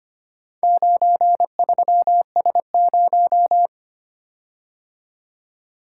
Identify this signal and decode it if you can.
Morse code, heard as a keyed tone, about 25 wpm, 700 Hz. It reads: 93S0
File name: practice / 25wpm / Four Letter-Number Combinations - 25wpm